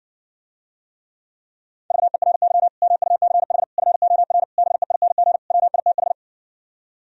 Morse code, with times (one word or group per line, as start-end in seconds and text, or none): 1.9-2.7 VERY
2.8-3.6 MUCH
3.8-4.4 FOR
4.6-5.4 BEING
5.5-6.1 WITH